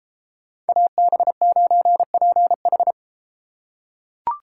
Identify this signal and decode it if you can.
Morse code, heard as a keyed tone, about 32 wpm, 700 Hz. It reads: AB9PH